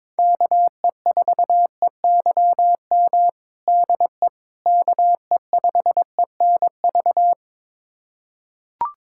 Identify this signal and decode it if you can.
Morse code, heard as a keyed tone, about 22 wpm, 700 Hz. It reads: KE4EYM DE KE5ENV
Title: USA Callsign DE Callsign - 6 Characters - 22wpm